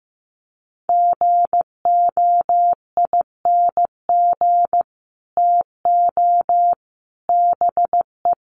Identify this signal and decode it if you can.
Morse code, heard as a keyed tone, about 15 wpm, 700 Hz. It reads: GOING TO BE
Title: Sets of 4 Words 15wpm